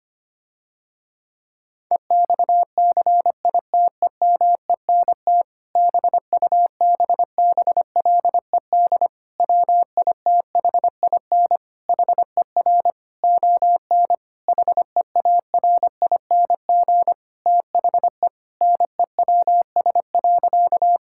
Morse code, heard as a keyed tone, about 25 wpm, 700 Hz. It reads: EXCITEMENT BUBBLED WITHIN HER ON HEARING THE NEWS.
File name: practice / 25wpm / Difficult Sentences with Hints - 25wpm